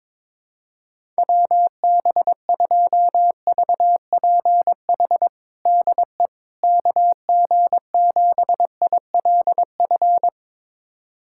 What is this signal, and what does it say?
Morse code, heard as a keyed tone, about 22 wpm, 700 Hz. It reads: WB2VPH DE KG7ILF
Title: USA Callsign DE Callsign 22wpm